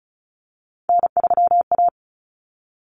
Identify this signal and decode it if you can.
Morse code, heard as a keyed tone, about 34 wpm, 700 Hz. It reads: N3A
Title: Common Cut Numbers - 3-Digit 35wpm